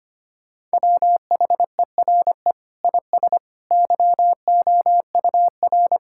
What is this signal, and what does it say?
Morse code, heard as a keyed tone, about 25 wpm, 700 Hz. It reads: WHERE IS YOUR
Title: Sentences from Top 1600 Words 25wpm